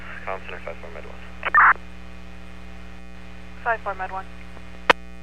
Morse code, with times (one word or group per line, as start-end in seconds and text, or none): none